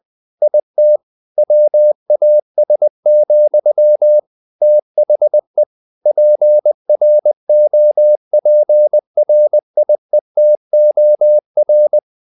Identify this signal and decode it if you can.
Morse code, heard as a keyed tone, about 20 wpm, 600 Hz. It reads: IT WAS, THE PROPRIETOR
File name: the old man and the sea part 2 20wpm